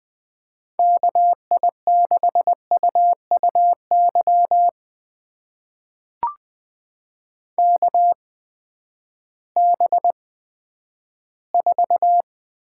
Morse code, heard as a keyed tone, about 20 wpm, spaced as for 11 wpm, 700 Hz. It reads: KI6UUY KB4
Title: USA Call Signs - 6 Characters - 8x Character Spacing - 20wpm